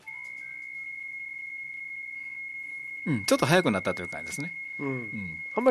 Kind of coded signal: PSK63